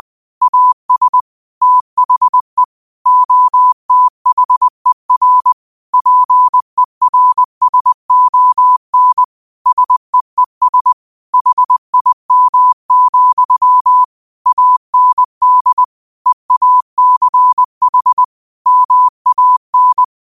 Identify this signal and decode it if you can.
Morse code, heard as a keyed tone, about 20 wpm, 1000 Hz. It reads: AS THE OTHER PERSON SEES HIM, AND EACH MAN
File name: qotdfiles / 20wpm